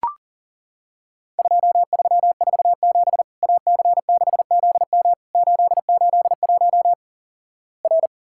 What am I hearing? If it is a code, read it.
Morse code, heard as a keyed tone, about 37 wpm, 700 Hz. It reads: 2347 AC6ZM 881 R
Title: Worked All Europe DX Contest - QTC Exchange - 40wpm